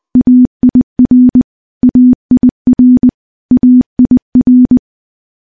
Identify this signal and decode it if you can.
Morse code, heard as a keyed tone, about 20 wpm, 250 Hz. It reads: AIR AIR AIR
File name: morse-training-20WPM